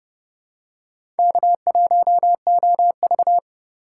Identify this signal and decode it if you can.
Morse code, heard as a keyed tone, about 29 wpm, 700 Hz. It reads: K1OV
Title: USA Call Signs - 4 Characters - 2x Character Spacing - 30wpm